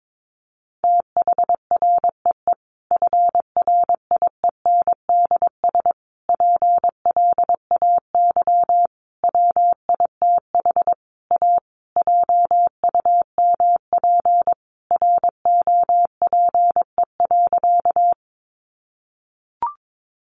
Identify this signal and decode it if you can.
Morse code, heard as a keyed tone, about 22 wpm, 700 Hz. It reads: THREE FRIENDS PLAY WITH A JUMP ROPE.
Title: Easy Phrases 22wpm